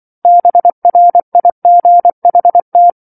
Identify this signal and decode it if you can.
Morse code, heard as a keyed tone, about 24 wpm, 700 Hz. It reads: BRIGHT
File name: 07-Chapter-07 24 WPM